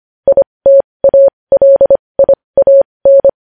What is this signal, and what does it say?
Morse code, heard as a keyed tone, about 25 wpm, spaced as for 20 wpm, 550 Hz. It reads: ITALIAN
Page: news-headlines-in-morse-code-at-20-wpm